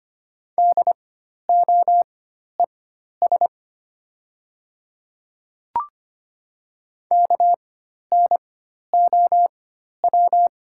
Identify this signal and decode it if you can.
Morse code, heard as a keyed tone, about 25 wpm, spaced as for 12 wpm, 700 Hz. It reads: DOES KNOW
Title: 4x Character Spacing - Top 200 Words - 25wpm